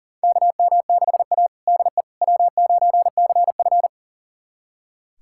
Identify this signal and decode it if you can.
Morse code, heard as a keyed tone, about 40 wpm, 700 Hz. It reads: KM6A DE W9CF
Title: USA Callsign DE Callsign - 4 Characters - 40wpm